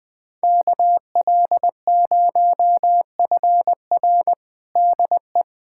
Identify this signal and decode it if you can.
Morse code, heard as a keyed tone, about 21 wpm, 700 Hz. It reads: KL0FR DE
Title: USA Callsign DE Callsign - 5 Characters - 20wpm